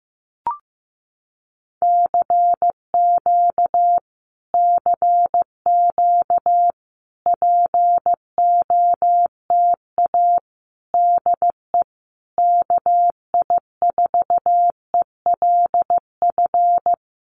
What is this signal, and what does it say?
Morse code, heard as a keyed tone, about 15 wpm, 700 Hz. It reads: CQ CQ POTA DE KI4ELF
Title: Common QSO Phrases Encore 15wpm